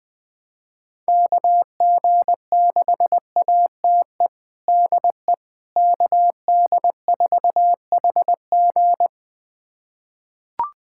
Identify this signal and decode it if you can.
Morse code, heard as a keyed tone, about 20 wpm, 700 Hz. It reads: KG6ATE DE KD4HG
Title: USA Callsign DE Callsign 20wpm